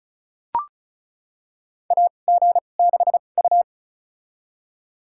Morse code, heard as a keyed tone, about 34 wpm, spaced as for 25 wpm, 700 Hz. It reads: AG6U
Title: USA Call Signs - 4 Characters - 2x Character Spacing - 35wpm